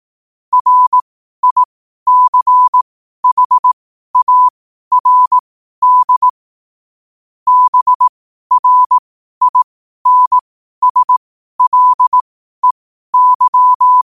Morse code, heard as a keyed tone, about 18 wpm, spaced as for 12 wpm, 1000 Hz. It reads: RICHARD BRINSLEY
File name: qotdfiles / 12wpm